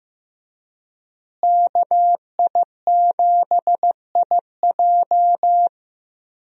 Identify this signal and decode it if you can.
Morse code, heard as a keyed tone, about 15 wpm, 700 Hz. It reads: KI7IJ
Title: USA Call Signs - 5 Characters - Repeated 3x - 15wpm